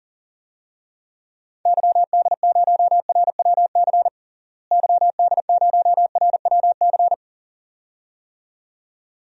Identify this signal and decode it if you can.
Morse code, heard as a keyed tone, about 40 wpm, 700 Hz. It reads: YD0RWC YD0RWC
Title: DXCC-100 Call Signs - 6 Characters - Repeated 2x - 40wpm